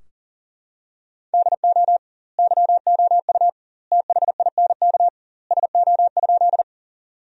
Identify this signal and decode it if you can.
Morse code, heard as a keyed tone, about 40 wpm, 700 Hz. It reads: DO YOU THINK SO?
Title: 2x Word-Spacing - Sentences from Top 200 Words - 40wpm